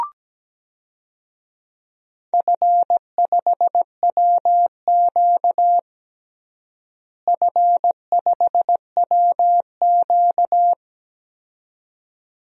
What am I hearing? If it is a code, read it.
Morse code, heard as a keyed tone, about 17 wpm, 700 Hz. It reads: F5WQ F5WQ
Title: DXCC-100 Call Signs - 4 Characters - Repeated 2x - 17wpm